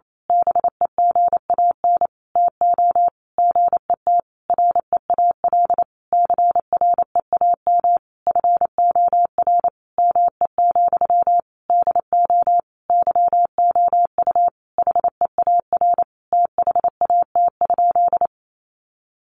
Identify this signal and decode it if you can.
Morse code, heard as a keyed tone, about 28 wpm, 700 Hz. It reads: BEGAN TO GET REAL CREAM FOR ME, DO YOU HEAR THAT?
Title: Sentences from Top 1500 Words 28wpm